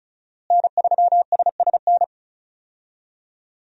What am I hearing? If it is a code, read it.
Morse code, heard as a keyed tone, about 36 wpm, 700 Hz. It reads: N3SSN
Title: USA Call Signs - 5 Characters - Repeated 3x - 35wpm